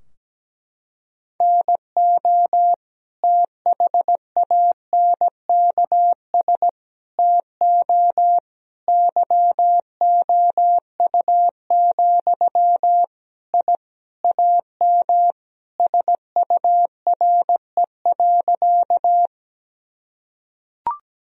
Morse code, heard as a keyed tone, about 17 wpm, 700 Hz. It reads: NO THANKS TO YOU, I AM SURE.
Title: Sentences from Top 1000 Words - Review Part 3 - 17wpm